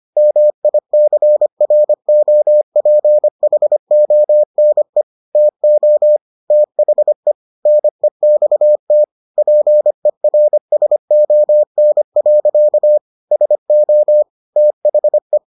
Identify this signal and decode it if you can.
Morse code, heard as a keyed tone, about 25 wpm, 600 Hz. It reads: MICROPHONE TO THE NEXT PERSON. SO THE